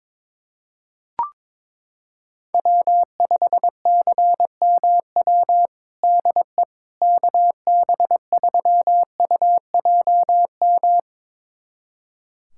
Morse code, heard as a keyed tone, about 22 wpm, 700 Hz. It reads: W5CMW DE KB3UJM
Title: USA Callsign DE Callsign 22wpm